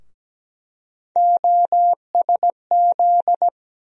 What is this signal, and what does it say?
Morse code, heard as a keyed tone, about 17 wpm, 700 Hz. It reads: OSZ
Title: USA Call Sign Postfixes - 3 Letters 17wpm